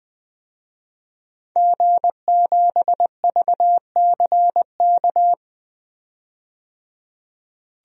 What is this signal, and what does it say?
Morse code, heard as a keyed tone, about 20 wpm, 700 Hz. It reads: G7VCK
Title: DXCC-100 Call Signs - 5 Characters - 20wpm